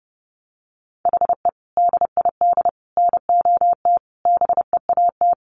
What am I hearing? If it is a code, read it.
Morse code, heard as a keyed tone, about 30 wpm, 700 Hz. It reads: HE DID NOT BEAT
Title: Sentences from Top 1300 Words 30wpm